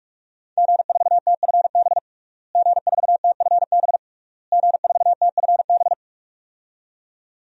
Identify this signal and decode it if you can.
Morse code, heard as a keyed tone, about 46 wpm, 700 Hz. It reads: G4TFB G4TFB G4TFB
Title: DXCC-100 Call Signs - 5 Characters - Repeated 3x - 45wpm